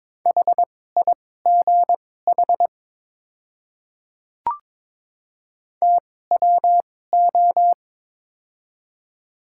Morse code, heard as a keyed tone, about 23 wpm, spaced as for 16 wpm, 700 Hz. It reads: HIGH TWO
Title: 2x Character Spacing - Top 200 Words - 22wpm